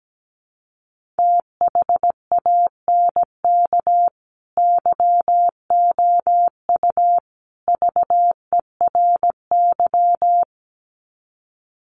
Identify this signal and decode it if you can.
Morse code, heard as a keyed tone, about 17 wpm, 700 Hz. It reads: THANK YOU VERY
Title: Sets of 3 Words 17wpm